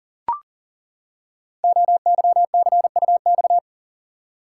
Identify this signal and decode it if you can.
Morse code, heard as a keyed tone, about 40 wpm, 690 Hz. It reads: OYCUX
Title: Code Groups - 5 Letter:Numbers - 40wpm